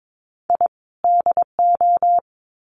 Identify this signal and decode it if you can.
Morse code, heard as a keyed tone, about 22 wpm, 700 Hz. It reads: I DO
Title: Sets of 2 Words 22wpm